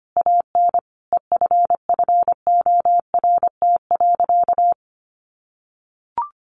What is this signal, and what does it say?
Morse code, heard as a keyed tone, about 25 wpm, 700 Hz. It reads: AN EFFORT.
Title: Sentences from Top 1200 Words 25wpm